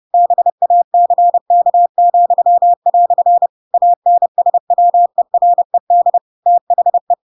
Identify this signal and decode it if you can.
Morse code, heard as a keyed tone, about 30 wpm, 700 Hz. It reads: BACK," ANSWERED THE
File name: Pinocchio / Pinocchio ch6 30 30 ik1hge